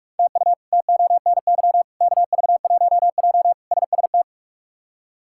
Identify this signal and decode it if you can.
Morse code, heard as a keyed tone, about 46 wpm, 700 Hz. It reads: TU TONY KV1J SST
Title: K1USN SST QSO Practice - 45wpm